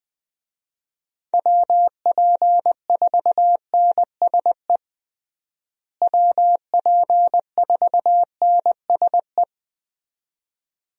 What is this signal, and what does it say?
Morse code, heard as a keyed tone, about 20 wpm, 700 Hz. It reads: WP4NSE WP4NSE
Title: USA Call Signs - 6 Characters - Repeated 2x - 20wpm